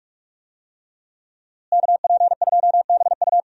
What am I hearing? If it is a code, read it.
Morse code, heard as a keyed tone, about 46 wpm, 700 Hz. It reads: KP2BU
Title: USA Call Signs - 5 Characters - Repeated 3x - 45wpm